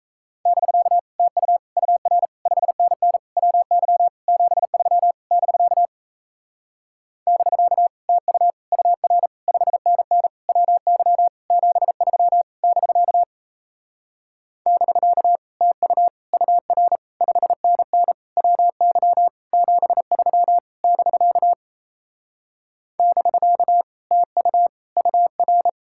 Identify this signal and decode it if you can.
Morse code, heard as a keyed tone, about 38 wpm, 700 Hz. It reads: <BK> TU UR 5NN WY 73 <BK> <BK> TU UR 5NN WY 73 <BK> <BK> TU UR 5NN WY 73 <BK> <BK> TU UR